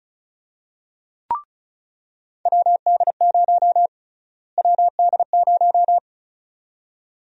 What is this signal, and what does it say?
Morse code, heard as a keyed tone, about 34 wpm, 700 Hz. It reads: WD0 WD0
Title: USA Call Sign Prefixes - Repeated 2x - 35wpm